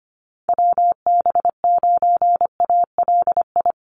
Morse code, heard as a keyed tone, about 25 wpm, 700 Hz. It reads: WB9ALI
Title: USA Call Signs Repeated 3x 25wpm